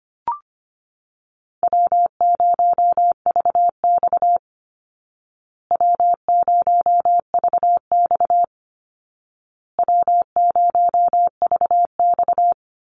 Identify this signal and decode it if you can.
Morse code, heard as a keyed tone, about 25 wpm, 700 Hz. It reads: W0VX W0VX W0VX
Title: Call Signs Repeated 2x - 25wpm